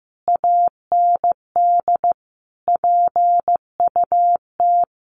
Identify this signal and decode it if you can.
Morse code, heard as a keyed tone, about 14 wpm, 700 Hz. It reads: AND PUT
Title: A Sound Of Thunder 15wpm